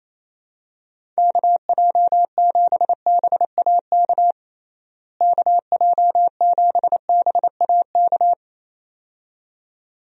Morse code, heard as a keyed tone, about 28 wpm, 700 Hz. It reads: KJ7BAK KJ7BAK
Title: USA Call Signs - 6 Characters - Repeated 2x - 28wpm